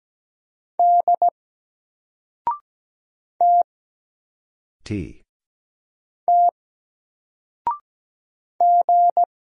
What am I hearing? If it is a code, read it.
Morse code, heard as a keyed tone, about 17 wpm, 700 Hz. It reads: D T T G
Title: Single Letter-Number Alt 17wpm